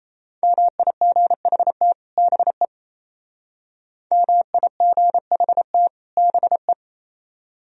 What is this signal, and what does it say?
Morse code, heard as a keyed tone, about 36 wpm, 700 Hz. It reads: MIGHT BE MIGHT BE